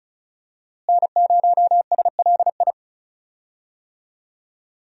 Morse code, heard as a keyed tone, about 34 wpm, 700 Hz. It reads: N0SLI
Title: USA Call Signs - 5 Characters - Repeated 2x - 35wpm